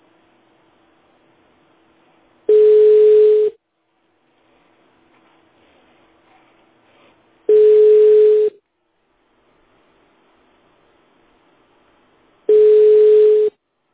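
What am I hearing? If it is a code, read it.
Ringback tone (Europe, 425 Hz).
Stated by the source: Phone call recorded on the petrosvyaz mdpbx PBX.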